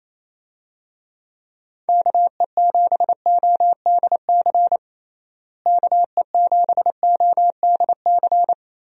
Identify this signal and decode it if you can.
Morse code, heard as a keyed tone, about 28 wpm, 700 Hz. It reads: KE7ODC KE7ODC
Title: USA Call Signs - 6 Characters - Repeated 3x - 28wpm